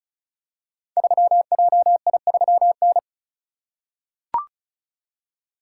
Morse code, heard as a keyed tone, about 36 wpm, 700 Hz. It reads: 3JI3N
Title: Code Groups - 5 Letter:Numbers - 35wpm